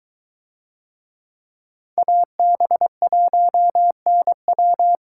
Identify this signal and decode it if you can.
Morse code, heard as a keyed tone, about 23 wpm, 700 Hz. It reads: AB1NW